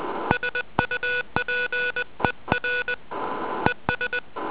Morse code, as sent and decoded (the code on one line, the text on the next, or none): ... ..- .--. .-. / . ...
SUPER ES